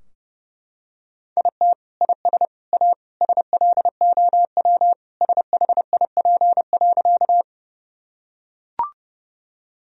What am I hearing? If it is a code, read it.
Morse code, heard as a keyed tone, about 32 wpm, 700 Hz. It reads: IT IS A SLOW SHIP.
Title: Sentences from Top 1000 Words - Review Part 2 - 30wpm